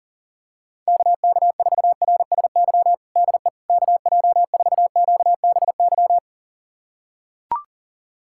Morse code, heard as a keyed tone, about 40 wpm, 700 Hz. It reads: KK4RSY DE KJ4QBY
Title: USA Callsign DE Callsign - 6 Characters - 40wpm